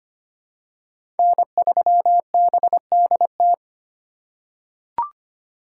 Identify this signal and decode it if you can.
Morse code, heard as a keyed tone, about 25 wpm, 700 Hz. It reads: N3BDT
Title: USA Call Signs - 5 Characters - 4x Character Spacing - 25wpm